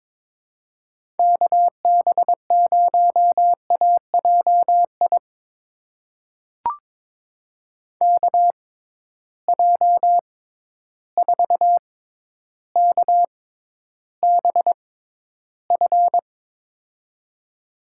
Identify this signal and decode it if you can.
Morse code, heard as a keyed tone, about 22 wpm, spaced as for 11 wpm, 700 Hz. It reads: KB0AJI KJ4KBF
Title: USA Call Signs - 6 Characters - 6x Character Spacing - 22wpm